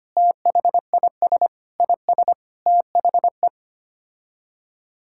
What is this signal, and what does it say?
Morse code, heard as a keyed tone, about 25 wpm, 700 Hz. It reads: THIS IS THE
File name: Sets of 9 Words - 25wpm